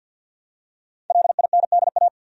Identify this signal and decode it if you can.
Morse code, heard as a keyed tone, about 47 wpm, 700 Hz. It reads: LINDA
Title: USA Common Names - Max 5 Letters - 50wpm